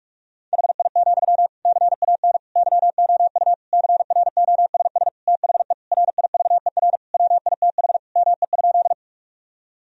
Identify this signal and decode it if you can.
Morse code, heard as a keyed tone, about 46 wpm, 700 Hz. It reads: HI, CAN YOU CROSS THE RIVER WITH ME?
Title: Sentences from Top 1000 Words - Review Part 3 - 45wpm